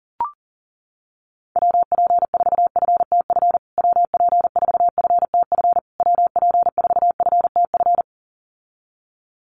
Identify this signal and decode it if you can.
Morse code, heard as a keyed tone, about 40 wpm, 700 Hz. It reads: WP4FTF WP4FTF WP4FTF
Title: USA Call Signs Repeated 3x 40wpm